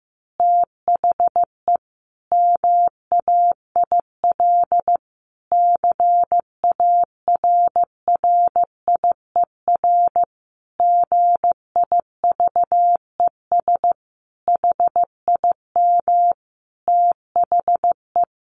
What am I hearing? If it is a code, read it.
Morse code, heard as a keyed tone, about 15 wpm, 700 Hz. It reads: THE MAIL CARRIER GIVES HIM THE
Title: Additional Easy Phrases 15wpm